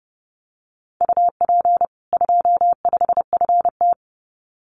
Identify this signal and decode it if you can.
Morse code, heard as a keyed tone, about 30 wpm, 700 Hz. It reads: UP 25FT
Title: Common QSO Phrases 30wpm